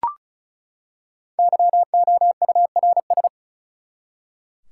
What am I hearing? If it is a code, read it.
Morse code, heard as a keyed tone, about 34 wpm, 700 Hz. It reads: YOURS
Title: Five Letter Words - 35wpm